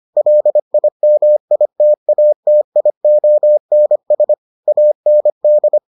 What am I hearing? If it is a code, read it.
Morse code, heard as a keyed tone, about 25 wpm, 600 Hz. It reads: LIMITATIONS AND